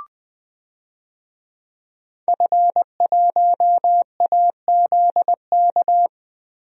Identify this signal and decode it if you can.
Morse code, heard as a keyed tone, about 20 wpm, 700 Hz. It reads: F1AZK
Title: DXCC-100 Call Signs - 5 Characters - Repeated 2x - 20wpm